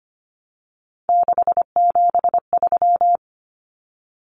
Morse code, heard as a keyed tone, about 25 wpm, 700 Hz. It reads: 673
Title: Common Cut Numbers - 3-Digit 25wpm